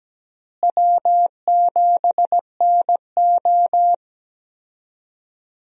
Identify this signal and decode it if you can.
Morse code, heard as a keyed tone, about 17 wpm, 700 Hz. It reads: W7NO
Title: USA Call Signs - 4 Characters - Repeated 2x - 17wpm